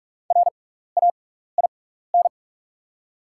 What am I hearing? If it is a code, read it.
Morse code, heard as a keyed tone, about 45 wpm, spaced as for 15 wpm, 690 Hz. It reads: RAIN
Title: 6x Character Spacing - Top 500 Words - 45wpm